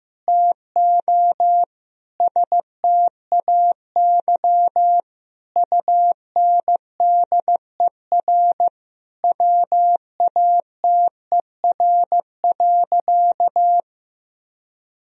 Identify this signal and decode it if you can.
Morse code, heard as a keyed tone, about 15 wpm, 700 Hz. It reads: TO STAY UNDER WATER.
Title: Sentences from Top 1000 Words - Review Part 2 - 15wpm